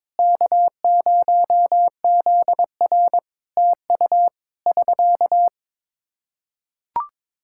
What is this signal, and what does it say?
Morse code, heard as a keyed tone, about 22 wpm, 700 Hz. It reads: K0ZR TU <SK>
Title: K1USN SST QSO Practice 22wpm